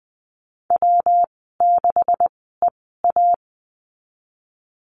Morse code, heard as a keyed tone, about 20 wpm, spaced as for 15 wpm, 700 Hz. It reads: W6EA
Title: 147 Lesson 24g - Letter y - Call Signs 20wpm